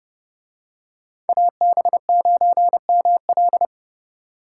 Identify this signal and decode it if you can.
Morse code, heard as a keyed tone, about 29 wpm, 700 Hz. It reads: AB9ML